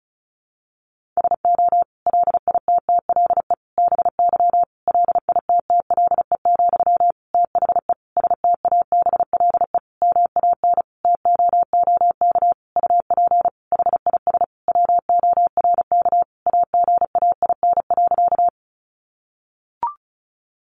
Morse code, heard as a keyed tone, about 34 wpm, 700 Hz. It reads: SO LITTLE BY LITTLE, THE STABLE MAN TOOK UP HIS WORK AGAIN.
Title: Sentences from Top 1500 Words 35wpm